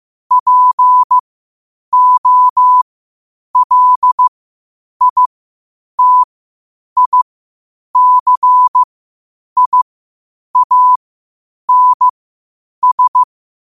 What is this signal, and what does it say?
Morse code, heard as a keyed tone, about 15 wpm, spaced as for 8 wpm, 1000 Hz. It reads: POLITICIANS
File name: qotdfiles / 8wpm